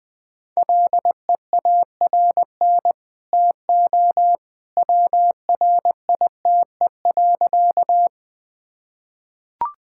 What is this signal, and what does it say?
Morse code, heard as a keyed tone, about 21 wpm, 700 Hz. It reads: LEARN TO WRITE.
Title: Easy Sentences with Hints - 20wpm